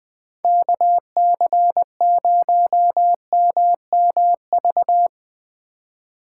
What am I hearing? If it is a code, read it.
Morse code, heard as a keyed tone, about 22 wpm, 700 Hz. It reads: KC0MMV
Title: Speed Racing - USA 2x3 Call Signs 20wpm